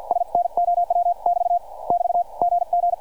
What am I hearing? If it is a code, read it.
Morse code, heard as a keyed tone, about 50 wpm, 700 Hz, with radio noise. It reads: SNOW <BT> <BT> GG